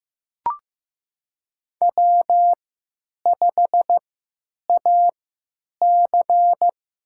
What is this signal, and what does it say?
Morse code, heard as a keyed tone, about 15 wpm, spaced as for 9 wpm, 700 Hz. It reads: W5AC
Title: USA Call Signs - 3x Character Spacing - 15wpm